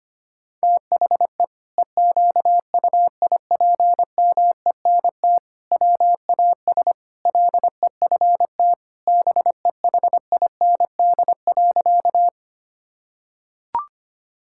Morse code, heard as a keyed tone, about 25 wpm, 700 Hz. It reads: THE EQUIPMENT WAS LEFT BEHIND.
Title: Sentences from Top 1000 Words - Review Part 5 - 25wpm